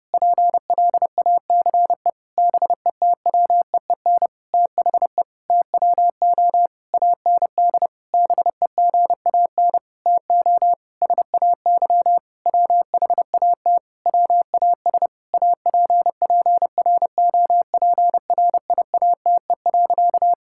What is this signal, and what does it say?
Morse code, heard as a keyed tone, about 30 wpm, 700 Hz. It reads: PLACE BETWEEN THE TWO AND BEGAN TO SAY WHAT WAS APPROPRIATE.
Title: Sentences from Top 1000 Words - Review Part 5 - 30wpm